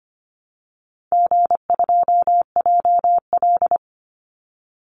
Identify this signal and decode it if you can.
Morse code, heard as a keyed tone, about 25 wpm, 700 Hz. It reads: G2JL
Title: Easy Call Signs - 25wpm